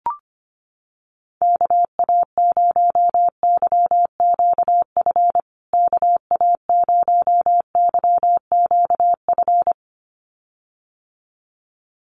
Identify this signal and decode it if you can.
Morse code, heard as a keyed tone, about 25 wpm, 700 Hz. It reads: KA0YQF KA0YQF
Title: USA Call Signs Repeated 2x 25wpm